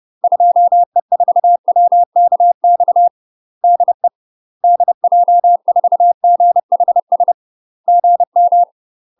Morse code, heard as a keyed tone, about 30 wpm, 700 Hz. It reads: 2E4WKX DE DJ4GHS GM
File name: UK MASTER QSOs TRAINING / 30WPM / UK MASTER QSOs 0012